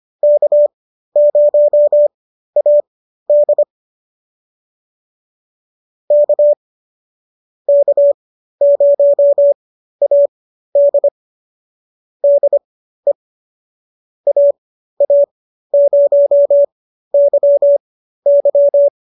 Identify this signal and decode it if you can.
Morse code, heard as a keyed tone, about 25 wpm, spaced as for 13 wpm, 570 Hz. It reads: K0AD K K0AD DE AA0YY